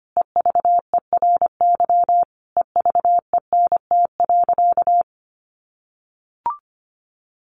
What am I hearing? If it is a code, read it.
Morse code, heard as a keyed tone, about 25 wpm, 700 Hz. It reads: EVERY EVENT.
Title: Sentences from Top 1400 Words 25wpm